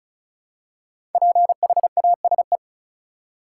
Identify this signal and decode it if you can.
Morse code, heard as a keyed tone, about 34 wpm, 700 Hz. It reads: PHASE
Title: Five Letter Words - 35wpm